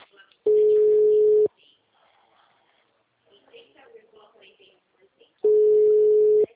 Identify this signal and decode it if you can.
Ringback tone (Europe, 425 Hz), with speech.